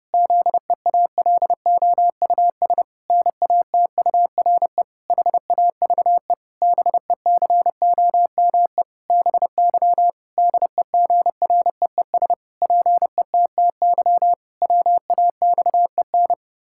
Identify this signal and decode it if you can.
Morse code, heard as a keyed tone, about 30 wpm, 700 Hz. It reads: ZEALOUS NATURE HAVE BECOME BY DEGREES PETTY WAXEN
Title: Pangram Challenge v2 - 30wpm